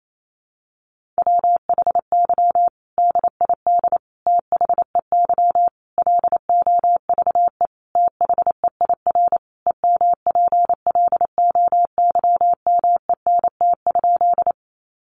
Morse code, heard as a keyed tone, about 28 wpm, 700 Hz. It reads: WHY DID THEY LOVE THEIR EMPLOYMENT?
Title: Sentences from Top 1500 Words 28wpm